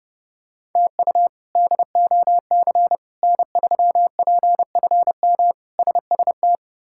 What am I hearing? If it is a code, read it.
Morse code, heard as a keyed tone, about 30 wpm, 700 Hz. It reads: TU DOC N3PFM SST
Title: K1USN SST QSO Practice - 30wpm